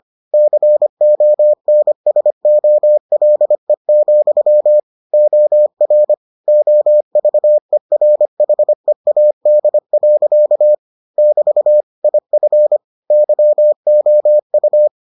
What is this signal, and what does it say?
Morse code, heard as a keyed tone, about 25 wpm, 600 Hz. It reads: CONSOLE, OR OVERHEAD. <BT> IF YOU